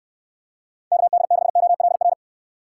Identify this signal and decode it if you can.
Morse code, heard as a keyed tone, about 69 wpm, 700 Hz. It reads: KD6OLR